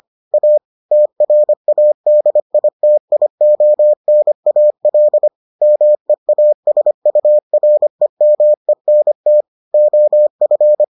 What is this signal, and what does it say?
Morse code, heard as a keyed tone, about 25 wpm, 600 Hz. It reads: A TRADITIONAL MEASUREMENT OF